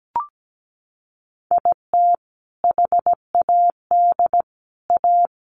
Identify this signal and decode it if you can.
Morse code, heard as a keyed tone, about 17 wpm, 700 Hz. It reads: IT HAD A
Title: Sentences from Top 1100 Words 17wpm